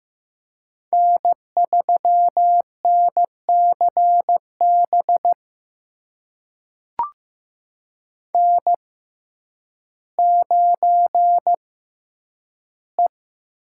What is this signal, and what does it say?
Morse code, heard as a keyed tone, about 15 wpm, spaced as for 9 wpm, 700 Hz. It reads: N3NCB N9E
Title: USA Call Signs - 5 Characters - 6x Character Spacing - 15wpm